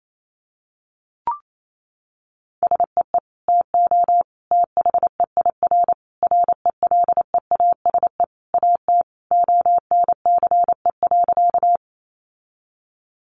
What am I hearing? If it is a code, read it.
Morse code, heard as a keyed tone, about 28 wpm, 700 Hz. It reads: SEE TO THEIR RELEASE AT ONCE.
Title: Sentences from Top 1100 Words 28wpm